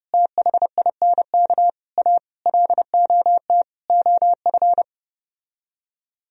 Morse code, heard as a keyed tone, about 31 wpm, 700 Hz. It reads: THINK A LOT OF